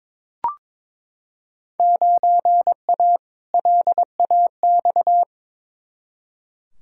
Morse code, heard as a keyed tone, about 22 wpm, 700 Hz. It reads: 9A LAX
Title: Field Day Exchanges Encore - 22wpm